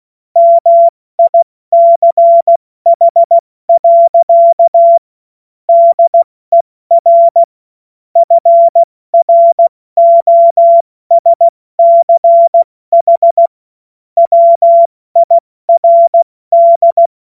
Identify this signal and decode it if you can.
Morse code, heard as a keyed tone, about 16 wpm, 680 Hz. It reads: MICH. DER FROSCH WIRD
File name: Morsekurs / Witze / Witz 05. April 2022 70 F80